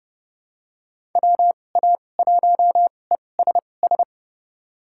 Morse code, heard as a keyed tone, about 30 wpm, spaced as for 22 wpm, 700 Hz. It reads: WA1ESS
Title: USA Call Signs - 2x Character Spacing - 30wpm